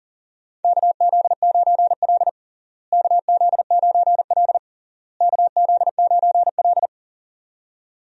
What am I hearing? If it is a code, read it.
Morse code, heard as a keyed tone, about 40 wpm, 700 Hz. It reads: KZ9L KZ9L KZ9L
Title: USA Call Signs - 4 Characters - Repeated 3x - 40wpm